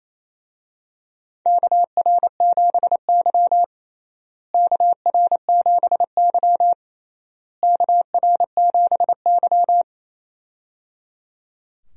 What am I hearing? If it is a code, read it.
Morse code, heard as a keyed tone, about 27 wpm, 700 Hz. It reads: KR7Y KR7Y KR7Y
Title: USA Call Signs - 4 Characters - Repeated 3x - 28wpm